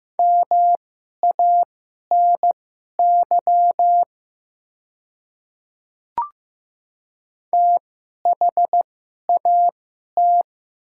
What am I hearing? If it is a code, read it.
Morse code, heard as a keyed tone, about 15 wpm, spaced as for 11 wpm, 690 Hz. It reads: MANY THAT
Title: 2x Character Spacing - Top 100 Words - 15wpm